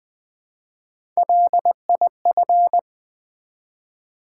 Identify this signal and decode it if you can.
Morse code, heard as a keyed tone, about 20 wpm, 700 Hz. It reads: LIF
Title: Build A Word - 1 Letter At A Time - 4-Letter Words - 20wpm